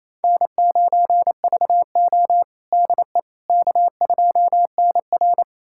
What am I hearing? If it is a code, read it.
Morse code, heard as a keyed tone, about 28 wpm, 700 Hz. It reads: N9VO DE K2NR
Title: USA Callsign DE Callsign - 4 Characters - 28wpm